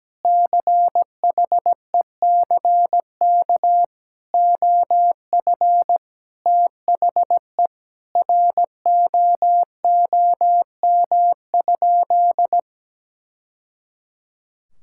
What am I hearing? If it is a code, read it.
Morse code, heard as a keyed tone, about 17 wpm, 700 Hz. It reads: CHECK OF THE ROOM?
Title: Sentences from Top 1000 Words - Review Part 6 - 17wpm